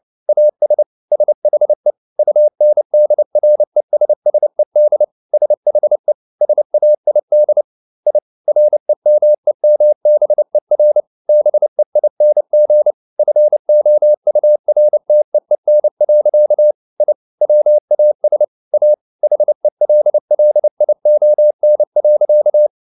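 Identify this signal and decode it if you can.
Morse code, heard as a keyed tone, about 29 wpm, 600 Hz. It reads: AS SHE UNDRESSED SHE SAID I REMEMBER BEING FOURTEEN. I WAS A HELLION.
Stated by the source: Morse code at 29 wpm, sending Ken Follett's novel Never in English, from CW med Gnister.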